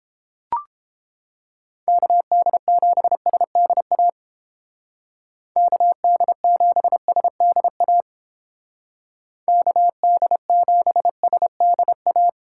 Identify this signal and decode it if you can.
Morse code, heard as a keyed tone, about 29 wpm, 700 Hz. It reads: KD7SDA KD7SDA KD7SDA